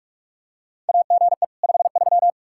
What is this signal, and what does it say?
Morse code, heard as a keyed tone, about 46 wpm, 700 Hz. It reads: AGE 53
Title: K1USN SST QSO Practice 45wpm